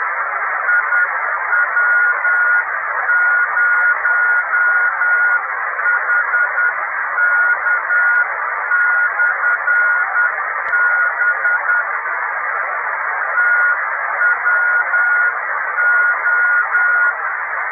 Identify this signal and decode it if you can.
Morse code, heard as a keyed tone, about 10 wpm, 1440 Hz, with radio noise. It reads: IW0HK/B TWO
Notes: Beacon amatoriale italiano 28322 KHz